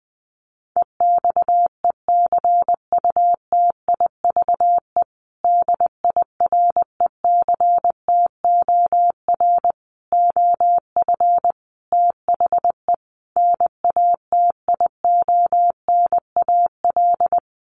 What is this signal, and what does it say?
Morse code, heard as a keyed tone, about 20 wpm, 700 Hz. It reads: EXECUTIVE DIRECTOR OF THE NATIONAL
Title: Sets of 5 Words Encore2 20wpm